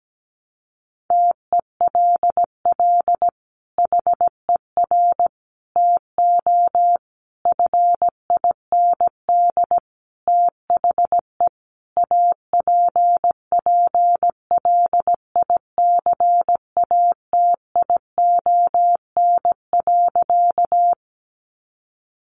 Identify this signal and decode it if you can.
Morse code, heard as a keyed tone, about 17 wpm, 700 Hz. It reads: TELL HER TO FIND THE APPLICATION.
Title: Sentences from Top 1200 Words 17wpm